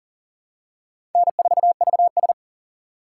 Morse code, heard as a keyed tone, about 40 wpm, 700 Hz. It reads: N4VS
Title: USA Call Signs - 4 Characters - 4x Character Spacing - 40wpm